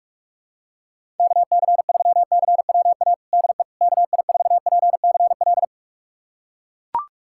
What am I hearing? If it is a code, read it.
Morse code, heard as a keyed tone, about 42 wpm, 700 Hz. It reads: KC3CWA DE KI4PCL
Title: USA Callsign DE Callsign 45wpm